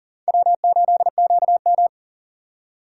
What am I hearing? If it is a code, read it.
Morse code, heard as a keyed tone, about 51 wpm, 700 Hz. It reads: W8QM